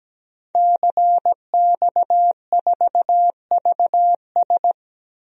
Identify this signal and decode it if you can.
Morse code, heard as a keyed tone, about 17 wpm, 700 Hz. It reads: CX4VS
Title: DXCC-100 Call Signs - 5 Characters - Repeated 3x - 17wpm